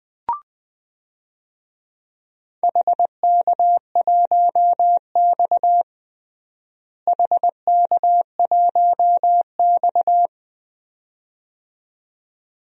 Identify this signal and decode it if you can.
Morse code, heard as a keyed tone, about 21 wpm, 700 Hz. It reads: HK1X HK1X
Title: DXCC-100 Call Signs - 4 Characters - Repeated 2x - 20wpm